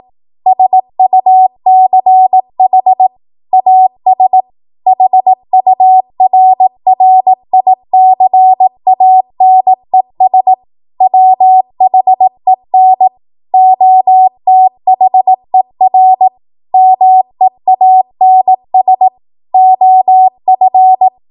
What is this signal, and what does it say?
Morse code, heard as a keyed tone, about 18 wpm, 750 Hz. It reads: SUCH AS HURRICANES WHEN OTHER MEANS OF
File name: Morse / 18WPM